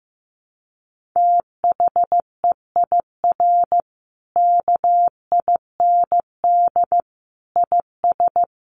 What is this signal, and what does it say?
Morse code, heard as a keyed tone, about 15 wpm, 700 Hz. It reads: THEIR KIND IS
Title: Sentences from Top 1500 Words 15wpm